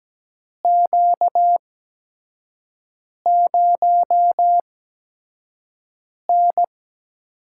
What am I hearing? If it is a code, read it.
Morse code, heard as a keyed tone, about 17 wpm, spaced as for 5 wpm, 700 Hz. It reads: Q0N
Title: USA Call Signs - 8x Character Spacing - 17wpm